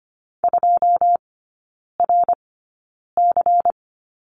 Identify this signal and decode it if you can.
Morse code, heard as a keyed tone, about 25 wpm, spaced as for 9 wpm, 700 Hz. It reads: 2RC
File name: course / 25:10wpm / 093 Lesson 16e - Letter c - Three Characters 25wpm